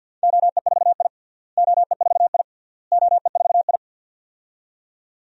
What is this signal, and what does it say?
Morse code, heard as a keyed tone, about 47 wpm, 700 Hz. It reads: OE4I OE4I OE4I
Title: DXCC-100 Call Signs - 4 Characters - Repeated 3x - 50wpm